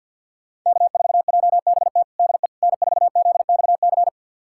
Morse code, heard as a keyed tone, about 50 wpm, 700 Hz. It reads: K4JBT DE N4ZXC
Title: USA Callsign DE Callsign - 5 Characters - 50wpm